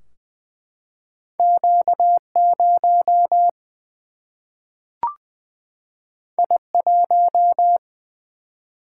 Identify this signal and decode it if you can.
Morse code, heard as a keyed tone, about 20 wpm, 700 Hz. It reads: Q 0 I 1
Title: Single-Letter Single-Number 20wpm